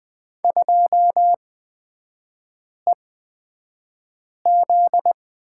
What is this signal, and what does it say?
Morse code, heard as a keyed tone, about 20 wpm, spaced as for 5 wpm, 700 Hz. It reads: 2EZ
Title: 243 Lesson 39e - Letter z - Three Characters 20wpm